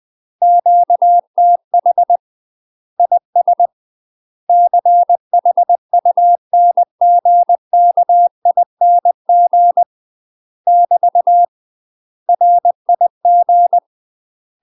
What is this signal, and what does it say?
Morse code, heard as a keyed tone, about 20 wpm, 700 Hz. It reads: QTH IS CHUNGKING <BT> RIG